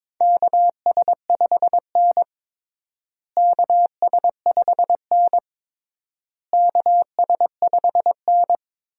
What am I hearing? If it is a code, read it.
Morse code, heard as a keyed tone, about 22 wpm, 700 Hz. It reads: KS5N KS5N KS5N
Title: USA Call Signs - 4 Characters - Repeated 3x - 22wpm